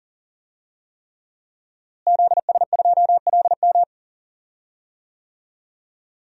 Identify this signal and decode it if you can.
Morse code, heard as a keyed tone, about 40 wpm, 700 Hz. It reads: ZS2LM
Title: DXCC-100 Call Signs - 5 Characters - Repeated 3x - 40wpm